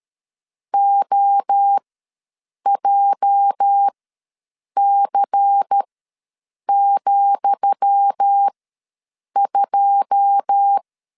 Morse code, heard as a keyed tone, about 13 wpm, spaced as for 7 wpm, 800 Hz. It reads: OJC,2
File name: LEZIONI / 10 SIGNS / 10D COMA ws 12wpm 3618